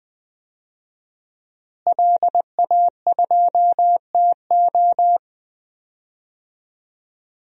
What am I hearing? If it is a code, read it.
Morse code, heard as a keyed tone, about 20 wpm, 700 Hz. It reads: LA2TO
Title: DXCC-100 Call Signs - 5 Characters - Repeated 2x - 20wpm